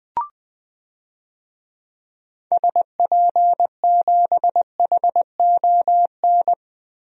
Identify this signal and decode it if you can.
Morse code, heard as a keyed tone, about 20 wpm, 700 Hz. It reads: SP7HON
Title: DXCC-100 Call Signs - 6 Characters - Repeated 2x - 20wpm